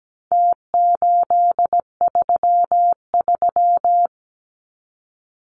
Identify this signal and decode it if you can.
Morse code, heard as a keyed tone, about 17 wpm, 700 Hz. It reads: T833
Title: Numbers - Contest Style - 4-Digit 17wpm